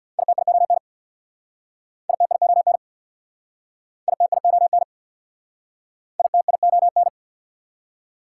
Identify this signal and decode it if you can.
Morse code, heard as a keyed tone, about 60 wpm, 700 Hz. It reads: ITION ITION ITION ITION
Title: Speed-Racing Prefixes and Suffixes 50wpm